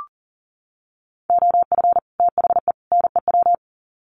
Morse code, heard as a keyed tone, about 39 wpm, 700 Hz. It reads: OF THE NEW
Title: Sets of 3 Words 40wpm